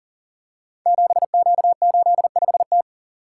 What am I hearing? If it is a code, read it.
Morse code, heard as a keyed tone, about 38 wpm, 700 Hz. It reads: 7Q85T
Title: Code Groups - 5 Letter:Numbers - 40wpm